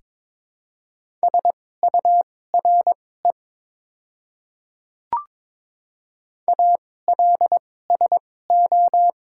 Morse code, heard as a keyed tone, about 22 wpm, spaced as for 16 wpm, 700 Hz. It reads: SURE ALSO
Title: 2x Character Spacing - Top 300 Words - 22wpm